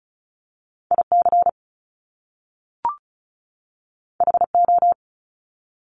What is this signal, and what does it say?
Morse code, heard as a keyed tone, about 36 wpm, 700 Hz. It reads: IC HO
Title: Top Two Letter Combinations in Words 35wpm